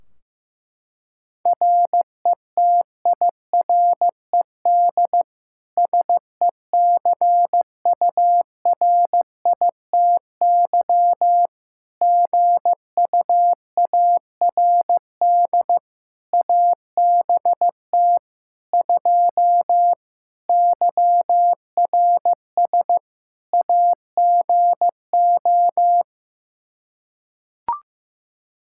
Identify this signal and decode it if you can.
Morse code, heard as a keyed tone, about 15 wpm, 700 Hz. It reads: RETIRED SECURITY GUARD ABT 2 YRS AGO
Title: Practice QSO - Exchange 3 - More Info - 15wpm